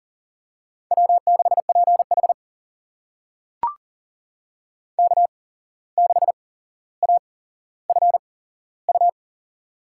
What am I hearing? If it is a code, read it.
Morse code, heard as a keyed tone, about 40 wpm, spaced as for 16 wpm, 700 Hz. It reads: W6PH K6AFU
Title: USA Call Signs - 8x Character Spacing - 40wpm